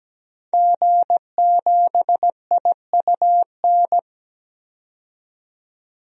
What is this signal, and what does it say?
Morse code, heard as a keyed tone, about 17 wpm, 700 Hz. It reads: G7IUN
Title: DXCC-100 Call Signs - 5 Characters - 17wpm